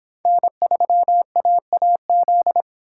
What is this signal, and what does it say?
Morse code, heard as a keyed tone, about 25 wpm, 700 Hz. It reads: N3AAZ